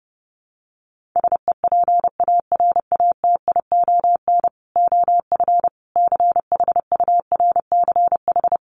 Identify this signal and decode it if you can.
Morse code, heard as a keyed tone, about 30 wpm, 700 Hz. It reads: SEPARATION OF CHURCH
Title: Sets of 5 Words 30wpm